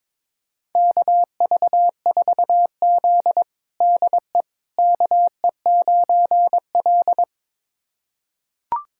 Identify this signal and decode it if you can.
Morse code, heard as a keyed tone, about 22 wpm, 700 Hz. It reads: KV4Z DE KE9L
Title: USA Callsign DE Callsign - 4 Characters - 22wpm